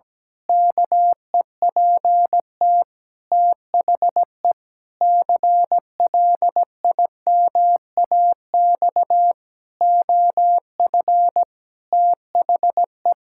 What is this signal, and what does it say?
Morse code, heard as a keyed tone, about 17 wpm, 700 Hz. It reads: KEPT THE CLIMAX OF THE
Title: Pangram Challenge v2 - 17wpm